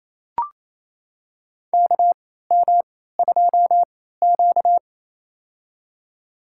Morse code, heard as a keyed tone, about 28 wpm, spaced as for 16 wpm, 700 Hz. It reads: KM2Q
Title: USA Call Signs - 4 Characters - 3x Character Spacing - 28wpm